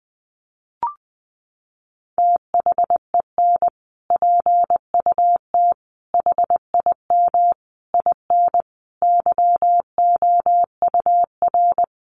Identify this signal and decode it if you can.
Morse code, heard as a keyed tone, about 20 wpm, 700 Hz. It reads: THEN PUT HIM IN YOUR
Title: Sentences from Top 1500 Words 20wpm